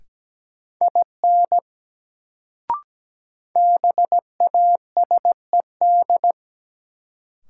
Morse code, heard as a keyed tone, about 17 wpm, 700 Hz. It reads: IN BASED
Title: Sets of 2 Words - New Word Drill - 17wpm